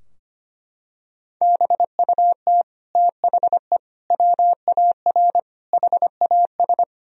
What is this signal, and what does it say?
Morse code, heard as a keyed tone, about 25 wpm, 700 Hz. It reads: BUT THE WAR HAS
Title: Sentences from Top 1000 Words - Review Part 2 - 25wpm